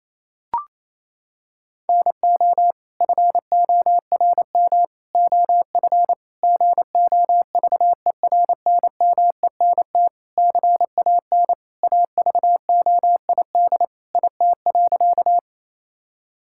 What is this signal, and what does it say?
Morse code, heard as a keyed tone, about 28 wpm, 700 Hz. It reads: NO FORM OF GOVERNMENT CAN AVOID IT.
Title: Sentences from Top 1000 Words - Review Part 6 - 28wpm